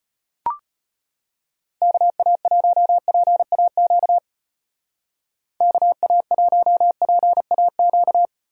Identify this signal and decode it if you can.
Morse code, heard as a keyed tone, about 30 wpm, 700 Hz. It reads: KA1PAQ KA1PAQ
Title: Speed Racing - USA 2x3 Call Signs 25wpm